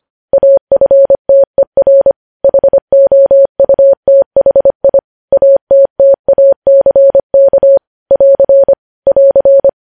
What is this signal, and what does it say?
Morse code, heard as a keyed tone, about 25 wpm, 550 Hz. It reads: AFTER HOUTHI ATTACK <AR> <AR>
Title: News Headlines at 25